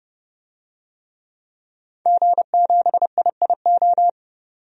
Morse code, heard as a keyed tone, about 32 wpm, 700 Hz. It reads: G7IIO
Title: DXCC-100 Call Signs - 5 Characters - 30wpm